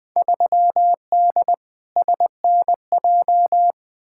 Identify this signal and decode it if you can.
Morse code, heard as a keyed tone, about 20 wpm, 700 Hz. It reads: 3D SNJ
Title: Field Day Exchanges Encore - 20wpm